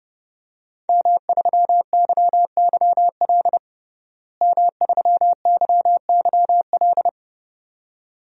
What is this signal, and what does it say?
Morse code, heard as a keyed tone, about 32 wpm, 700 Hz. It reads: M3YYL M3YYL
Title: DXCC-100 Call Signs - 5 Characters - Repeated 2x - 30wpm